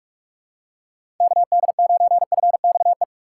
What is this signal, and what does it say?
Morse code, heard as a keyed tone, about 46 wpm, 700 Hz. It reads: KD9FXE
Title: USA Call Signs - 6 Characters - 3x Character Spacing - 45wpm